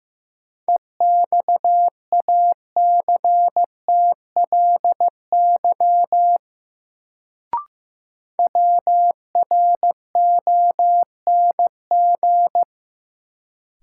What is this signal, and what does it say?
Morse code, heard as a keyed tone, about 15 wpm, 700 Hz. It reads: EXACTLY WRONG
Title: Sentences from Top 900 Words - New Word Drill - 15wpm